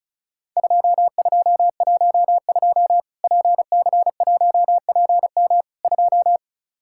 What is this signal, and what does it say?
Morse code, heard as a keyed tone, about 36 wpm, 700 Hz. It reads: 2212 PC1PM 2
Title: Worked All Europe DX Contest - QTC Exchange - 35wpm